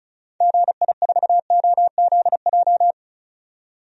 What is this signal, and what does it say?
Morse code, heard as a keyed tone, about 34 wpm, 700 Hz. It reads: GI4OZJ
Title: DXCC-100 Call Signs - 6 Characters - 35wpm